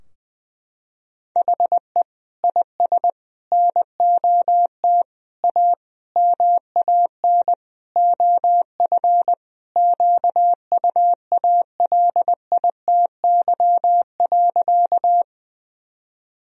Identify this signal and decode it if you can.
Morse code, heard as a keyed tone, about 20 wpm, 700 Hz. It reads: HE IS NOT A MAN OF QUALITY.
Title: Sentences from Top 1000 Words - Review Part 4 - 20wpm